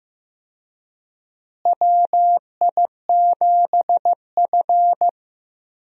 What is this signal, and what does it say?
Morse code, heard as a keyed tone, about 15 wpm, 700 Hz. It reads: WI7F
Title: USA Call Signs - 4 Characters - Repeated 3x - 15wpm